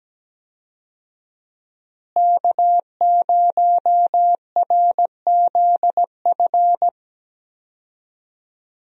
Morse code, heard as a keyed tone, about 17 wpm, 700 Hz. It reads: K0RZF
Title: USA Call Signs - 5 Characters - Repeated 2x - 17wpm